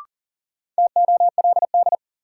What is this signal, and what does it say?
Morse code, heard as a keyed tone, about 40 wpm, 700 Hz. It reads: TOLD
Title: Sentences from Top 400 Words - New Word Drill - 40wpm